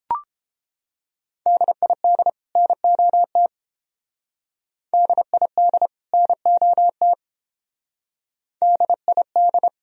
Morse code, heard as a keyed tone, about 30 wpm, 700 Hz. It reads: DID NOT DID NOT DID